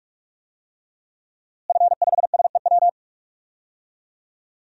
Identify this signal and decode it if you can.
Morse code, heard as a keyed tone, about 46 wpm, 700 Hz. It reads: F5SEW
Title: DXCC-100 Call Signs - 5 Characters - Repeated 2x - 45wpm